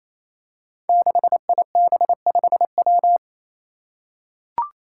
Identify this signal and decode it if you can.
Morse code, heard as a keyed tone, about 27 wpm, 700 Hz. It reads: 6IB5W
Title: Five Challenging Letter-Number Combinations 28wpm